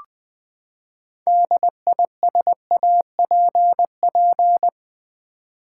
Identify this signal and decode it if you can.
Morse code, heard as a keyed tone, about 20 wpm, 700 Hz. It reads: DISAPP
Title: Build A Word - 6 Letters At A Time - 12-Letter Words - 20wpm